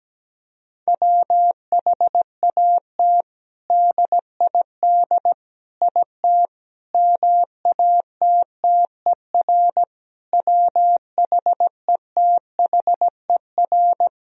Morse code, heard as a keyed tone, about 17 wpm, 700 Hz. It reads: WHAT DID IT MATTER WHETHER
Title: Sentences from Top 1000 Words - Review Part 3 - 17wpm